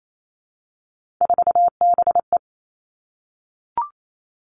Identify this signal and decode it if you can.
Morse code, heard as a keyed tone, about 28 wpm, 700 Hz. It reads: 4BE
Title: Three Challenging Letter-Number Combinations 28wpm